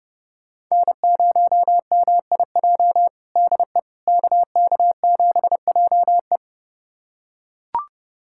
Morse code, heard as a keyed tone, about 30 wpm, 700 Hz. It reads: N0MIJ DE KK7JE
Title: USA Callsign DE Callsign - 5 Characters - 30wpm